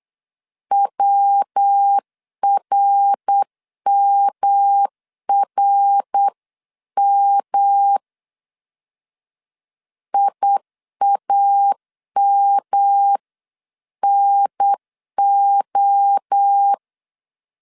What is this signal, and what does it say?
Morse code, heard as a keyed tone, about 8 wpm, 800 Hz. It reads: WRMRM IAMNO